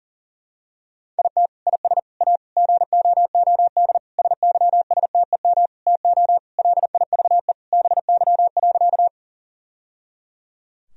Morse code, heard as a keyed tone, about 40 wpm, 700 Hz. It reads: IT IS A GOOD SYSTEM TO LIVE BY.
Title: Sentences from Top 1000 Words - Review Part 3 - 40wpm